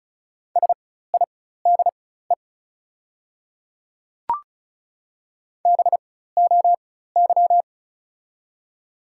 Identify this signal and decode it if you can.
Morse code, heard as a keyed tone, about 36 wpm, spaced as for 16 wpm, 700 Hz. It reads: SIDE BOY
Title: 4x Character Spacing - Top 200 Words - 35wpm